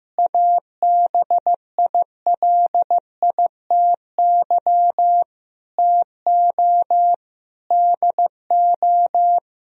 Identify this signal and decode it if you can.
Morse code, heard as a keyed tone, about 15 wpm, 700 Hz. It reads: ABILITY TO DO
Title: Sentences from Top 1000 Words - Review Part 3 - 15wpm